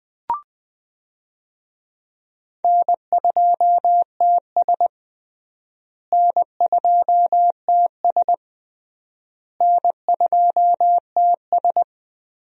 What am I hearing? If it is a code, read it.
Morse code, heard as a keyed tone, about 20 wpm, 700 Hz. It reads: N2TS N2TS N2TS
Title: USA Call Signs - 4 Characters - Repeated 3x - 20wpm